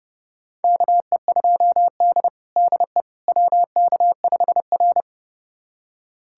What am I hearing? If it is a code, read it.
Morse code, heard as a keyed tone, about 30 wpm, 700 Hz. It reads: KE2D DE WK5R
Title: USA Callsign DE Callsign - 4 Characters - 30wpm